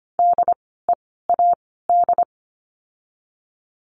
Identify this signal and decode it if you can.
Morse code, heard as a keyed tone, about 25 wpm, 700 Hz. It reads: D E A D
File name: course / 25:10wpm / 094 Lesson 16f - Letter c - Words 25wpm